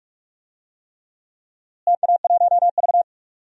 Morse code, heard as a keyed tone, about 46 wpm, 700 Hz. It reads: TA1V
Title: DXCC-100 Call Signs - 4 Characters - Repeated 3x - 45wpm